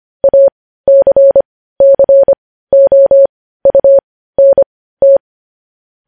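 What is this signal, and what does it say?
Morse code, heard as a keyed tone, about 25 wpm, spaced as for 15 wpm, 550 Hz. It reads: ACCOUNT
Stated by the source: Morse code news headlines at 15 WPM.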